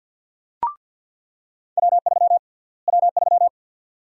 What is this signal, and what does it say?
Morse code, heard as a keyed tone, about 48 wpm, 700 Hz. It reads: W3 W3
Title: USA Call Sign Prefixes - Repeated 2x - 50wpm